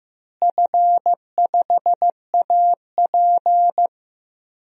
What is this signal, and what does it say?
Morse code, heard as a keyed tone, about 16 wpm, 700 Hz. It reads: F5AP
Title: DXCC-100 Call Signs - 4 Characters - Repeated 3x - 15wpm